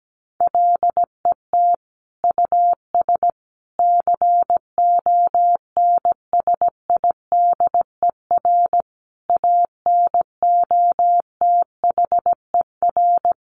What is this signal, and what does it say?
Morse code, heard as a keyed tone, about 17 wpm, 700 Hz. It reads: LET US CONSIDER ANOTHER
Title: Sentences from Top 600 Words 17wpm